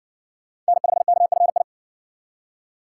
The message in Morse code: N5KFI

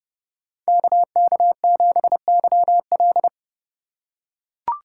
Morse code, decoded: KK7YL

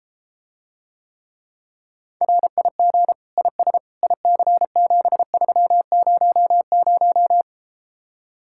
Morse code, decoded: RIG IS IC7300

